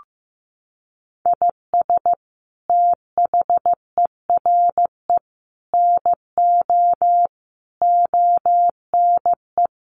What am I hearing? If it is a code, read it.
Morse code, heard as a keyed tone, about 15 wpm, 700 Hz. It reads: IS THERE NO ONE
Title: Sentences from Top 200 Words 15wpm